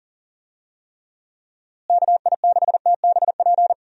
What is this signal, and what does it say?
Morse code, heard as a keyed tone, about 40 wpm, 700 Hz. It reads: KI6TBP
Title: USA Call Signs - 6 Characters - Repeated 2x - 40wpm